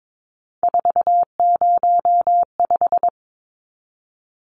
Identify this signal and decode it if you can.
Morse code, heard as a keyed tone, about 22 wpm, 700 Hz. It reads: 405
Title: Numbers - 3-Digit 22wpm